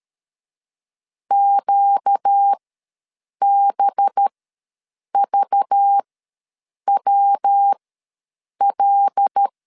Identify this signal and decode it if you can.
Morse code, heard as a keyed tone, about 13 wpm, spaced as for 7 wpm, 800 Hz. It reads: QBVWL
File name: LEZIONI / 5 LBVC / 5 LBV ws 12wpm 3015